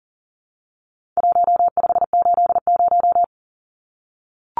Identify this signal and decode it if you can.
Morse code, heard as a keyed tone, about 40 wpm, 700 Hz. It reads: 1580
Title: Numbers - Contest Style - 4-Digit 40wpm